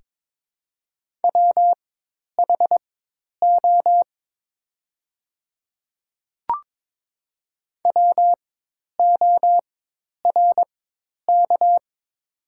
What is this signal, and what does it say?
Morse code, heard as a keyed tone, about 22 wpm, spaced as for 10 wpm, 700 Hz. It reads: WHO WORK
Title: 4x Character Spacing - Top 500 Words - 22wpm